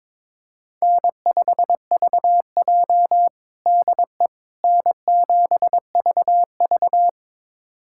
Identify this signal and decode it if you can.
Morse code, heard as a keyed tone, about 22 wpm, 700 Hz. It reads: N5VJ DE N7VV
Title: USA Callsign DE Callsign - 4 Characters - 22wpm